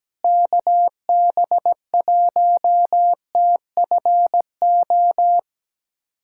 Morse code, heard as a keyed tone, about 17 wpm, 700 Hz. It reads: KB1TFO
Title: USA Callsign DE Callsign - 6 Characters - 17wpm